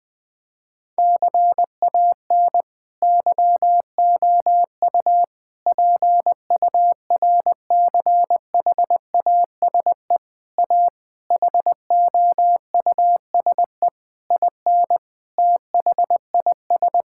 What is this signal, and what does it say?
Morse code, heard as a keyed tone, about 20 wpm, 700 Hz. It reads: CAN YOU PURCHASE A HOUSE IN THIS
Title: Sentences from Top 1000 Words - Review Part 5 - 20wpm